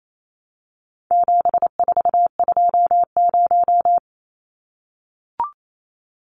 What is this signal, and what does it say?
Morse code, heard as a keyed tone, about 28 wpm, 700 Hz. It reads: 7420
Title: Numbers - 4-Digit 28wpm